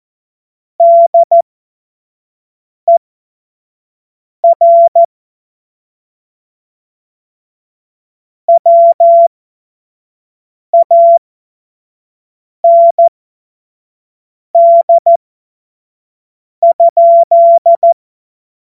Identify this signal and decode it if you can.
Morse code, heard as a keyed tone, about 14 wpm, spaced as for 5 wpm, 680 Hz. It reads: DER WAND?
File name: Morsekurs / Witze / Witz 23. Mai 2023 25 F70